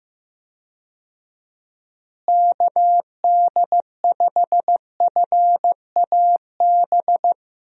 Morse code, heard as a keyed tone, about 15 wpm, 700 Hz. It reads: KD5FAB
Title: USA Call Signs - 6 Characters - Repeated 2x - 15wpm